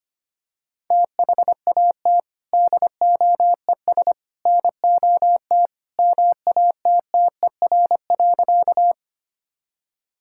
Morse code, heard as a keyed tone, about 25 wpm, 700 Hz. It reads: THAT DOES NOT MATTER.
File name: practice / 25wpm / Sentences with 3-Word Phrases - 25wpm